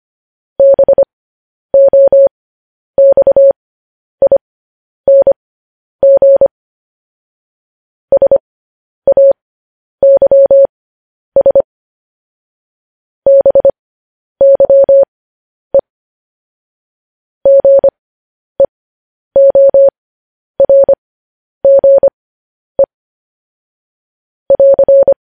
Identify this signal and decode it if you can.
Morse code, heard as a keyed tone, about 25 wpm, spaced as for 10 wpm, 550 Hz. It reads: BOXING SAYS BYE GEORGE <AR>
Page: news-headlines-in-morse-code-at-10-wpm